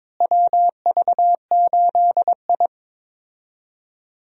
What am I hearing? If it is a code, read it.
Morse code, heard as a keyed tone, about 22 wpm, 700 Hz. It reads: WV8I
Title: USA Call Signs - 4 Characters - Repeated 3x - 22wpm